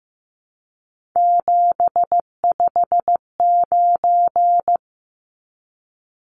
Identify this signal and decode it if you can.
Morse code, heard as a keyed tone, about 15 wpm, 700 Hz. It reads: 759
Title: Numbers - 3-Digit 15wpm